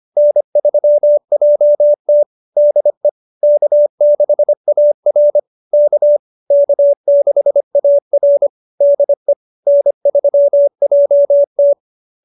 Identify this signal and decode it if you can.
Morse code, heard as a keyed tone, about 25 wpm, 590 Hz. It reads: N3JT DE K6AR K K6AR DE N3JT